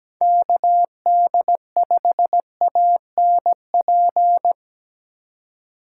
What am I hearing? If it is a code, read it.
Morse code, heard as a keyed tone, about 17 wpm, 700 Hz. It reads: KD5ANP
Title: USA Call Signs - 6 Characters - 4x Character Spacing - 17wpm